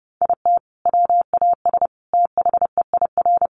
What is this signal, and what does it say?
Morse code, heard as a keyed tone, about 30 wpm, 700 Hz. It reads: IT WAS THEIR
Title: Sentences from Top 1500 Words 30wpm